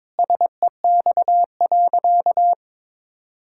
Morse code, heard as a keyed tone, about 22 wpm, 700 Hz. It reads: SEX.
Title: Sentences from Top 1600 Words 22wpm